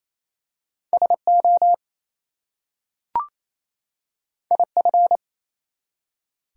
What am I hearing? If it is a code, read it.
Morse code, heard as a keyed tone, about 27 wpm, 700 Hz. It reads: SO IF
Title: Two Letter Words - 28wpm